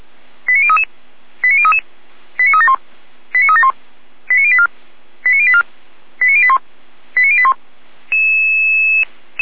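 Five-tone selective call (ZVEI), heard as quick runs of five tones, tones of about 70 ms. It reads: 89033 89033 89371 89371 89084 89084 89081 89081
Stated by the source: ZVEI five-tone selective call 89313 received on SDR, on the Feuerwehr Orlamünde (f85360) audio server.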